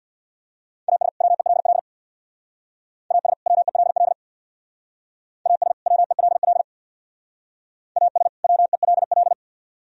Morse code, heard as a keyed tone, about 64 wpm, 700 Hz. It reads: AS WELL AS WELL AS WELL AS WELL